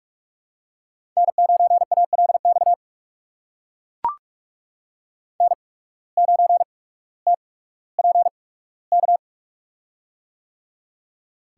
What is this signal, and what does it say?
Morse code, heard as a keyed tone, about 44 wpm, spaced as for 19 wpm, 700 Hz. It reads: N9ALX N9TPK
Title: USA Call Signs - 5 Characters - 8x Character Spacing - 45wpm